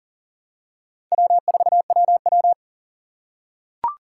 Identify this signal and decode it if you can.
Morse code, heard as a keyed tone, about 37 wpm, 700 Hz. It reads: W4WW
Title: USA Call Signs - 4 Characters - 2x Character Spacing - 40wpm